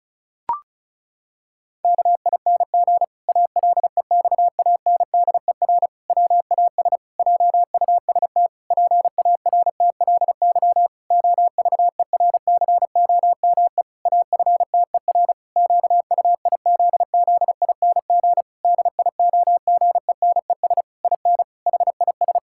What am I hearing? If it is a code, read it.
Morse code, heard as a keyed tone, about 36 wpm, 700 Hz. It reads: KING ALEXANDER WAS JUST PARTLY OVERCOME AFTER QUIZZING DIOGENES IN HIS
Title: Pangram Challenge v2 - 35wpm